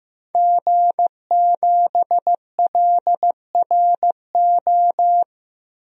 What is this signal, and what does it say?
Morse code, heard as a keyed tone, about 15 wpm, 700 Hz. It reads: G7LRO
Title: DXCC-100 Call Signs - 5 Characters - Repeated 3x - 15wpm